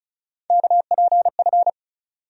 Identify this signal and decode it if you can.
Morse code, heard as a keyed tone, about 36 wpm, 700 Hz. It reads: KPF
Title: USA Call Sign Postfixes - 3 Letters 35wpm